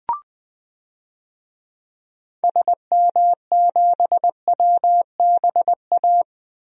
Morse code, heard as a keyed tone, about 20 wpm, 700 Hz. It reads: SM7WBA
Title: DXCC-100 Call Signs - 6 Characters - 20wpm